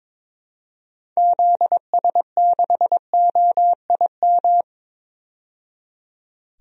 Morse code, heard as a keyed tone, about 22 wpm, 700 Hz. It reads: ZS6OIM
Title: DXCC-100 Call Signs - 6 Characters - 22wpm